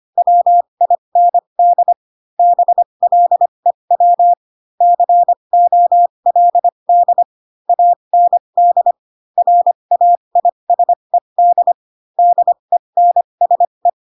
Morse code, heard as a keyed tone, about 25 wpm, spaced as for 22 wpm, 700 Hz. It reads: WIND BLEW COLD AND RAISED DENSE
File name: Pinocchio / Pinocchio ch6 18 25 ik1hge